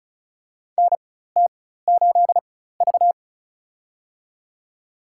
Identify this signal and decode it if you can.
Morse code, heard as a keyed tone, about 34 wpm, spaced as for 16 wpm, 700 Hz. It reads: NT8V
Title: USA Call Signs - 4 Characters - 4x Character Spacing - 35wpm